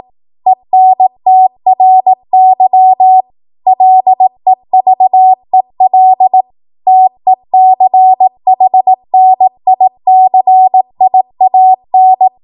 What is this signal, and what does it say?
Morse code, heard as a keyed tone, about 18 wpm, 750 Hz. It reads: ENTRY LEVEL TECHNICIAN
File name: Morse / 18WPM